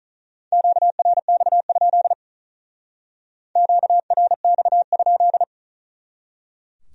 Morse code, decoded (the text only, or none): QRX? QRX?